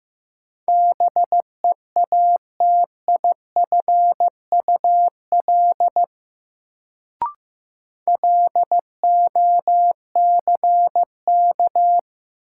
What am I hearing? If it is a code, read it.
Morse code, heard as a keyed tone, about 15 wpm, 700 Hz. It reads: BEATIFUL LOCK
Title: Sentences from Top 1300 Words - New Word Drill - 15wpm